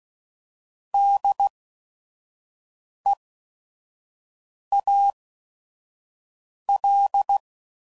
Morse code, heard as a keyed tone, about 16 wpm, spaced as for 5 wpm, 800 Hz. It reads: DEAL
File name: Prefixes / By First Letter / Ds